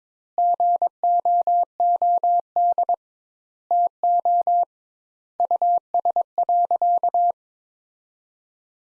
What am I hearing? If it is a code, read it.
Morse code, heard as a keyed tone, about 22 wpm, 700 Hz. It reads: GOOD TO US.
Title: Sentences Top 200 Words - 22wpm